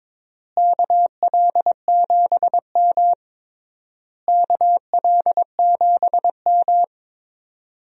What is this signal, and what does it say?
Morse code, heard as a keyed tone, about 22 wpm, 700 Hz. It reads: KL7M KL7M
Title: DXCC-100 Call Signs - 4 Characters - Repeated 3x - 22wpm